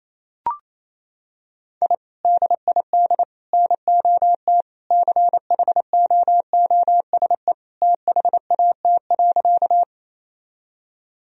Morse code, heard as a keyed tone, about 28 wpm, 700 Hz. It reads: I DID NOT CHOOSE THAT.
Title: Sentences from Top 1000 Words - Review Part 4 - 28wpm